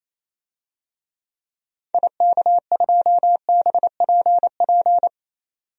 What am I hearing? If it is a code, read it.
Morse code, heard as a keyed tone, about 28 wpm, 700 Hz. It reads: IK2BPP
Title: DXCC-100 Call Signs - 6 Characters - 28wpm